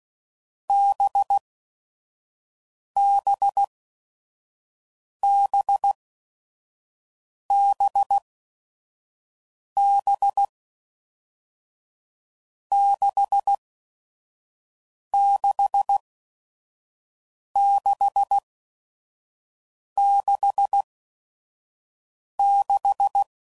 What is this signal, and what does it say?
Morse code, heard as a keyed tone, about 16 wpm, spaced as for 5 wpm, 800 Hz. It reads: BBBBB 66666
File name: Combinations / Difficult Combos / NDB NDB6